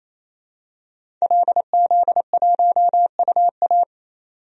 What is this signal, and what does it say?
Morse code, heard as a keyed tone, about 28 wpm, 700 Hz. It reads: LZ1UA
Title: DXCC-100 Call Signs - 5 Characters - 28wpm